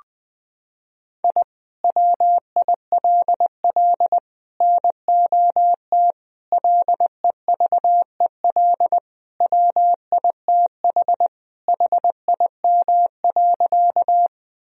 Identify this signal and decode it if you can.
Morse code, heard as a keyed tone, about 20 wpm, 700 Hz. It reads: I WILL NOT LEVEL WITH HIM.
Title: Sentences from Top 1000 Words - Review Part 4 - 20wpm